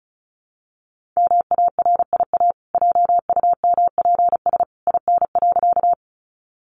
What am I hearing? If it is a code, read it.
Morse code, heard as a keyed tone, about 34 wpm, 700 Hz. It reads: MARIA JUMPS IN.
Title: Easy Phrases 35wpm